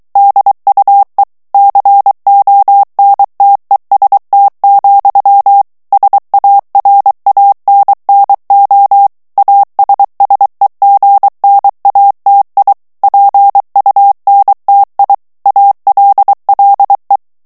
Morse code, heard as a keyed tone, about 23 wpm, 800 Hz. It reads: DUE CONTEST, SARANNO ASSEGNATI PUNTI ALLE
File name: ITALIEN texte clair 2 vitesse 22 mots minute